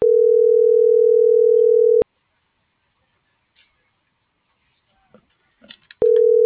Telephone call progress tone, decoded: ringback tone

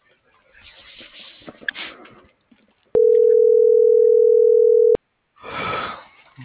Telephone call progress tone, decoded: ringback tone

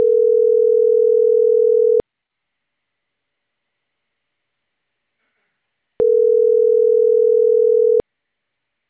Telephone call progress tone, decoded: ringback tone

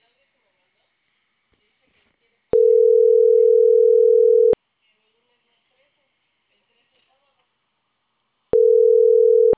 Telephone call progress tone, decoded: ringback tone

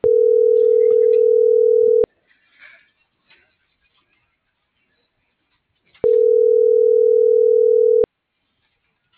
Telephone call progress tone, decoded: ringback tone

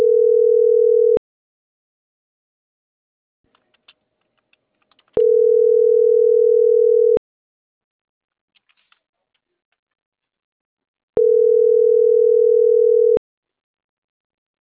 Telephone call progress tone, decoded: ringback tone

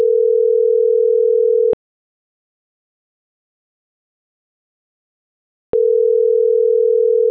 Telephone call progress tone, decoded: ringback tone

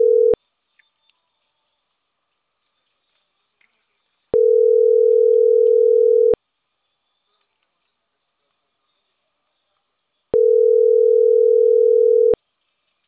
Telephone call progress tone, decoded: ringback tone

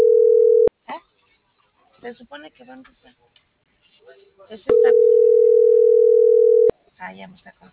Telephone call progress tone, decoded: ringback tone

